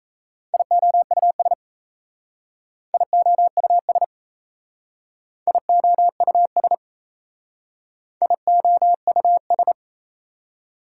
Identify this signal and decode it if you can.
Morse code, heard as a keyed tone, about 34 wpm, 690 Hz. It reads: IOUS IOUS IOUS IOUS
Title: Speed-Racing Prefixes and Suffixes 28wpm